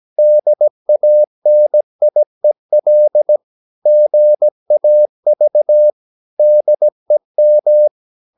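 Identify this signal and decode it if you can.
Morse code, heard as a keyed tone, about 17 wpm, 600 Hz. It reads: DANIEL GAV DEM
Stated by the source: Slow (QRS) Morse code at 16 wpm, from CW med Gnister.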